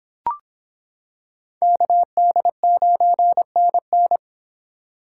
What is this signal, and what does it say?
Morse code, heard as a keyed tone, about 25 wpm, 700 Hz. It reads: KD9NN